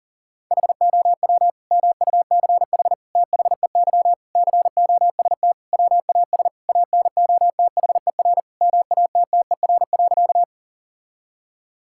Morse code, heard as a keyed tone, about 37 wpm, 700 Hz. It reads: HOW MUCH THEY COST WAS ANOTHER MATTER.
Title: Sentences from Top 1000 Words - Review Part 5 - 40wpm